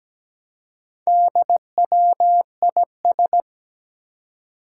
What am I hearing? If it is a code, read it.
Morse code, heard as a keyed tone, about 17 wpm, 700 Hz. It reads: DWIS
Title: Four Challenging Letter-Number Combinations 17wpm